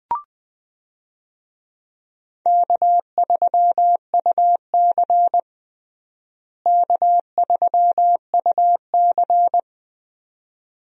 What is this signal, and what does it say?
Morse code, heard as a keyed tone, about 21 wpm, 700 Hz. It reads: K3UC K3UC
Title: USA Call Signs - 4 Characters - Repeated 2x - 20wpm